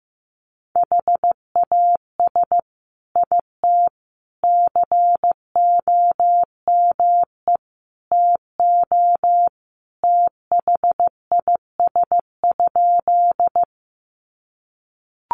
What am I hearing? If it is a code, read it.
Morse code, heard as a keyed tone, about 15 wpm, 700 Hz. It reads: HAS IT COME TO THIS?
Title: Sentences from Top 100 Words 15wpm